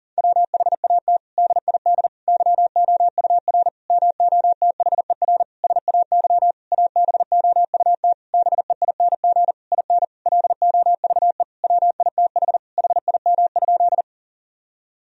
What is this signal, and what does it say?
Morse code, heard as a keyed tone, about 37 wpm, 700 Hz. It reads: WHAT DID YOUR MOTHER SAY ABOUT BEING IN LOVE WITH HIM?
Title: Sentences from Top 1000 Words - Review Part 3 - 40wpm